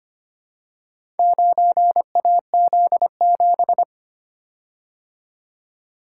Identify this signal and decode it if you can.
Morse code, heard as a keyed tone, about 25 wpm, 700 Hz. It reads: 9AZ7
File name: practice / 25wpm / Four Letter-Number Combinations - 25wpm